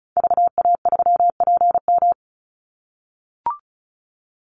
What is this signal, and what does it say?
Morse code, heard as a keyed tone, about 34 wpm, 700 Hz. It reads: VA3PM
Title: Call Signs - 35wpm